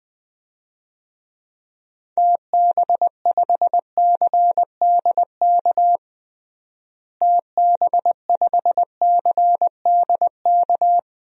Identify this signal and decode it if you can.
Morse code, heard as a keyed tone, about 20 wpm, 700 Hz. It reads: TB5CDK TB5CDK
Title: DXCC-100 Call Signs - 6 Characters - Repeated 2x - 20wpm